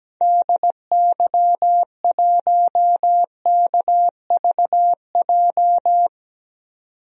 Morse code, heard as a keyed tone, about 17 wpm, 700 Hz. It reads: DY1KVJ
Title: DXCC-100 Call Signs - 6 Characters - 17wpm